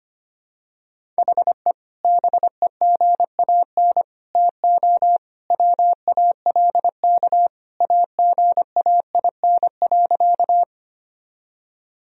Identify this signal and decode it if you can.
Morse code, heard as a keyed tone, about 25 wpm, 700 Hz. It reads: HE BEGAN TO WALK AGAIN.
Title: Sentences from Top 1000 Words - Review Part 1 - 25wpm